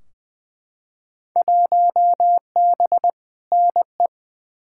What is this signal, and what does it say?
Morse code, heard as a keyed tone, about 20 wpm, 700 Hz. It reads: 1B NE
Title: Field Day Exchanges Encore - 20wpm